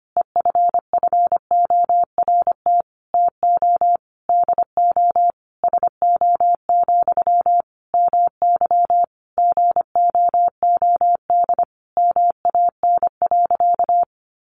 Morse code, heard as a keyed tone, about 25 wpm, 700 Hz. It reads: EFFORT TO DO SO, MY GOOD MAN.
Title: Sentences from Top 1200 Words 25wpm